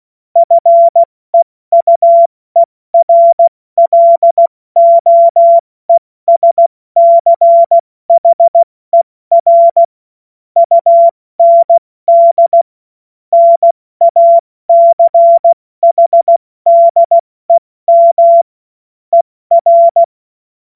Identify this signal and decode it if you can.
Morse code, heard as a keyed tone, about 16 wpm, 680 Hz. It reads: FEUERLOESCHER UND NACHDEM ER
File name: Morsekurs / Witze / Witz 14. Juni 2022 70 F80